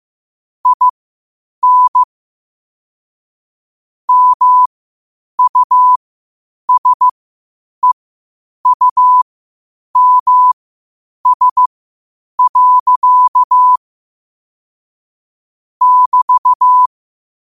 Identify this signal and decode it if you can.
Morse code, heard as a keyed tone, about 15 wpm, spaced as for 8 wpm, 1000 Hz. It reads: IN MUSEUMS. <BT>
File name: qotdfiles / 8wpm